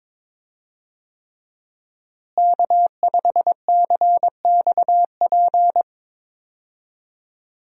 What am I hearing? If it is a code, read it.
Morse code, heard as a keyed tone, about 22 wpm, 700 Hz. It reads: K5CXP
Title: USA Call Signs - 5 Characters - Repeated 3x - 22wpm